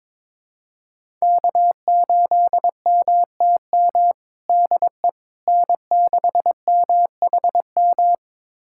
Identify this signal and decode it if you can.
Morse code, heard as a keyed tone, about 22 wpm, 700 Hz. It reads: K8MTM DE N6MHM
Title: USA Callsign DE Callsign - 5 Characters - 22wpm